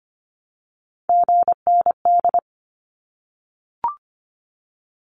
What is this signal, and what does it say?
Morse code, heard as a keyed tone, about 25 wpm, 700 Hz. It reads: GND
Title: Abbreviations Repeated 3x - 25wpm